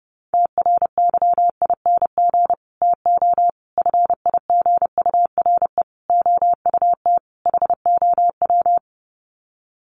Morse code, heard as a keyed tone, about 30 wpm, 700 Hz. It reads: TRYING TO FIGURE OUT HOW
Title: Sets of 5 Words Encore 30wpm